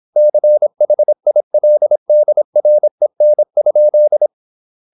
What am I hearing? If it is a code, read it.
Morse code, heard as a keyed tone, about 26 wpm, 600 Hz. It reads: CHILDREN?
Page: Never 26wpm | CW med Gnister